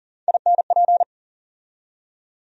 Morse code, heard as a keyed tone, about 40 wpm, 700 Hz. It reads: INP
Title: USA Call Sign Postfixes - 3 Letters 40wpm